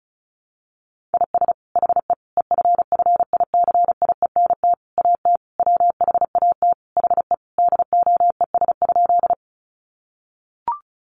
Morse code, heard as a keyed tone, about 34 wpm, 700 Hz. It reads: IS HE EFFICIENT AT WHAT HE DOES?
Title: Sentences from Top 1300 Words 35wpm